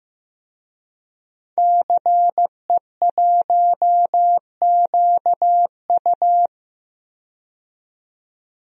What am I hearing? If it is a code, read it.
Morse code, heard as a keyed tone, about 15 wpm, 700 Hz. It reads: CE1QU
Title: DXCC-100 Call Signs - 5 Characters - 15wpm